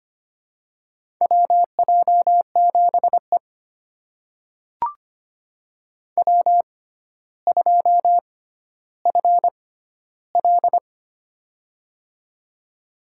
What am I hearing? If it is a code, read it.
Morse code, heard as a keyed tone, about 25 wpm, spaced as for 13 wpm, 700 Hz. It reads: WJ7E W2FL
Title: USA Call Signs - 4 Characters - 6x Character Spacing - 25wpm